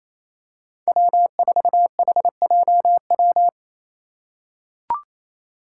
Morse code, heard as a keyed tone, about 28 wpm, 700 Hz. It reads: W4HJW
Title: Five Challenging Letter-Number Combinations 28wpm